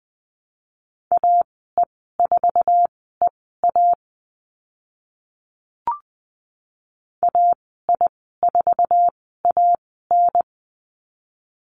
Morse code, heard as a keyed tone, about 20 wpm, spaced as for 14 wpm, 700 Hz. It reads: AE4EA AI4AN
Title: 076 Lesson 13g - Letter L - Call Signs 20wpm